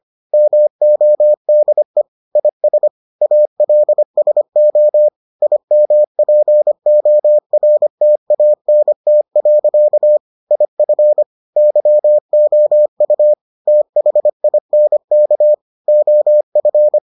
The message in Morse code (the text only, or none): MODE IS ALSO IMPORTANT. IF YOU THINK OF